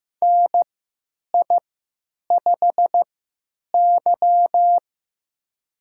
Morse code, heard as a keyed tone, about 15 wpm, spaced as for 9 wpm, 700 Hz. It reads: NI5Y
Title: USA Call Signs - 4 Characters - 3x Character Spacing - 15wpm